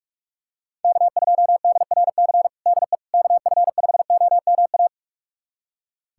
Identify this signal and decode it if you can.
Morse code, heard as a keyed tone, about 42 wpm, 700 Hz. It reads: K2DRC DE KF5OMA
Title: USA Callsign DE Callsign 45wpm